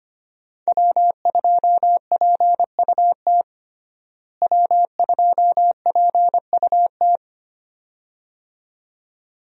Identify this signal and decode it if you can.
Morse code, heard as a keyed tone, about 25 wpm, 700 Hz. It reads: W2PUT W2PUT
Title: USA Call Signs - 5 Characters - Repeated 2x - 25wpm